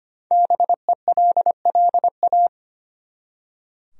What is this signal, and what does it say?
Morse code, heard as a keyed tone, about 25 wpm, 700 Hz. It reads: BELLA
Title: USA Common Names - Max 5 Letters - 25wpm